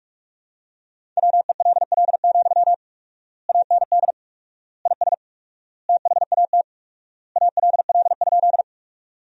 Morse code, heard as a keyed tone, about 42 wpm, 700 Hz. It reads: WELL, AND IS THAT ALL?
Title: 4x Word-Spacing - Sentences from Top 200 Words - 45wpm